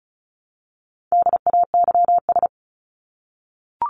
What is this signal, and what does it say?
Morse code, heard as a keyed tone, about 34 wpm, 700 Hz. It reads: DAYS
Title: ES:S Suffix - Top 100 Words with Base Words 35wpm